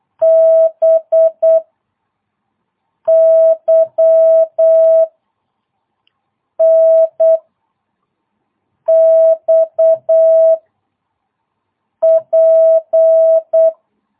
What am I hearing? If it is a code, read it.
Morse code, heard as a keyed tone, about 8 wpm, 660 Hz. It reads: B Y N X P